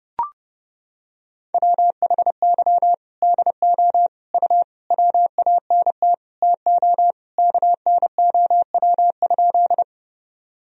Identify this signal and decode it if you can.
Morse code, heard as a keyed tone, about 30 wpm, 700 Hz. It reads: WHY DO U WANT TO KNOW?
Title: Sentences from Top 200 Words and Abbreviations 30wpm